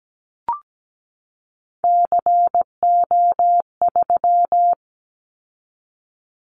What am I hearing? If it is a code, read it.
Morse code, heard as a keyed tone, about 17 wpm, 700 Hz. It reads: CO3
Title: Callsign Prefixes for Top 100 DXCC - 17wpm